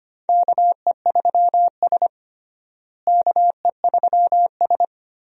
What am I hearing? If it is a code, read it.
Morse code, heard as a keyed tone, about 25 wpm, 700 Hz. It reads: KE3S KE3S
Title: USA Call Signs - 4 Characters - Repeated 2x - 25wpm